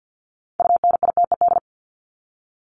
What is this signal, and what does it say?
Morse code, heard as a keyed tone, about 50 wpm, 700 Hz. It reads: UNITED
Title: ED Suffix - Top 100 Words with Base Words 50wpm